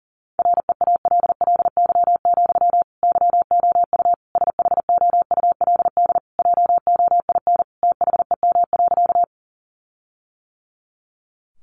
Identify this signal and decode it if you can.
Morse code, heard as a keyed tone, about 40 wpm, 700 Hz. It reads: REALLY, YOU SHOULD JOIN THEM.
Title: Sentences from Top 1400 Words 40wpm